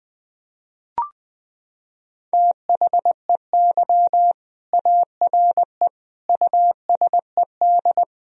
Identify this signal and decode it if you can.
Morse code, heard as a keyed tone, about 20 wpm, 700 Hz. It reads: THEY ARE USED
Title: Sentences from Top 1000 Words - Review Part 3 - 20wpm